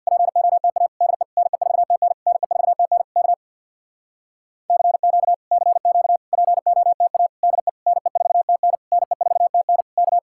Morse code, heard as a keyed tone, about 66 wpm, 700 Hz. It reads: POTA DE NE4TN NE4TN K CQ CQ POTA DE NE4TN NE4TN K